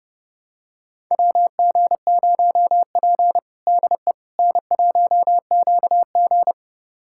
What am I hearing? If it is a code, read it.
Morse code, heard as a keyed tone, about 30 wpm, 700 Hz. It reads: WG0P DE N1QG
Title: USA Callsign DE Callsign - 4 Characters - 30wpm